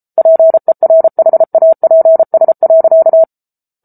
Morse code, heard as a keyed tone, about 34 wpm, 650 Hz. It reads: PERHAPS.
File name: Book Two / 07-Chapter 7 - 33 WPM